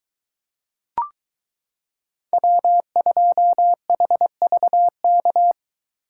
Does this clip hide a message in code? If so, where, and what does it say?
Yes: Morse code from 2.3 to 5.5 s, heard as a keyed tone, about 23 wpm, 700 Hz. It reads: W2HVK